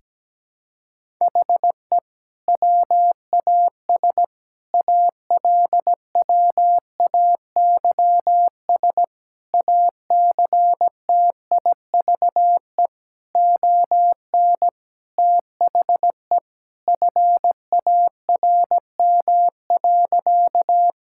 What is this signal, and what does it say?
Morse code, heard as a keyed tone, about 17 wpm, 700 Hz. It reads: HE WAS ALWAYS ACTIVE ON THE FARM.
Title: Sentences from Top 1000 Words - Review Part 6 - 17wpm